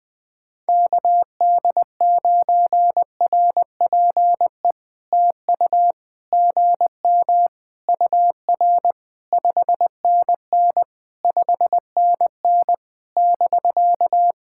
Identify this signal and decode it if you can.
Morse code, heard as a keyed tone, about 20 wpm, 700 Hz. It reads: KD9RPE TU GM UR 5NN 5NN <BK>
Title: POTA QSO Practice - 20wpm